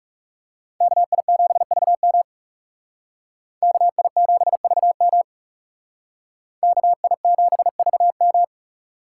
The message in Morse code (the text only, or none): KI7VM KI7VM KI7VM